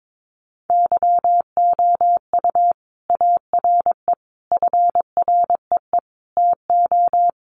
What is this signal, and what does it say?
Morse code, heard as a keyed tone, about 22 wpm, 700 Hz. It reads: YOU ARE FREE TO
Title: Sentences from Top 700 Words 22wpm